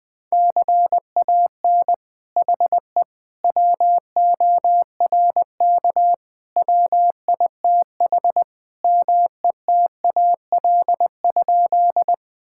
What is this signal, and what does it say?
Morse code, heard as a keyed tone, about 20 wpm, 700 Hz. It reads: CAN HE WORK WITH METAL?
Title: Sentences from Top 1000 Words - Review Part 6 - 20wpm